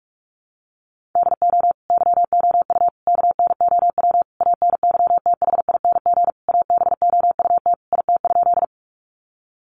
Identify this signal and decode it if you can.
Morse code, heard as a keyed tone, about 44 wpm, 700 Hz. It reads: DO YOU KNOW ANYTHING ABOUT IT?
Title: Sentences from Top 1000 Words 45wpm